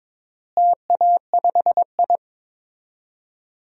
Morse code, heard as a keyed tone, about 23 wpm, 700 Hz. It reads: TA5I
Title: DXCC-100 Call Signs - 4 Characters - Repeated 3x - 22wpm